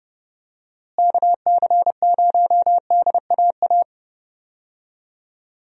Morse code, heard as a keyed tone, about 30 wpm, 700 Hz. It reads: KC0DAA
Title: USA Call Signs - 6 Characters - Repeated 3x - 30wpm